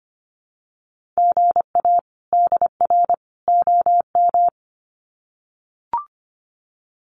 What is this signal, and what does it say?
Morse code, heard as a keyed tone, about 25 wpm, 700 Hz. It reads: GA DR OM
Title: Common QSO Phrases 25wpm